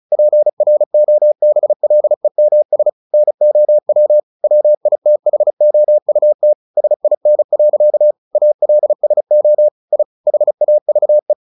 Morse code, heard as a keyed tone, about 35 wpm, 600 Hz. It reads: PROBLEMS NOW WITHOUT SIN. ALSO I HAVE
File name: the old man and the sea 2 35wpm